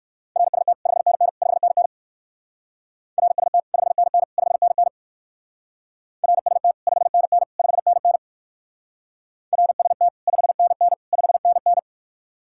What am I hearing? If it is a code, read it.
Morse code, heard as a keyed tone, about 58 wpm, 700 Hz. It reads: RST 5NN 5NN RST 5NN 5NN RST 5NN 5NN RST 5NN 5NN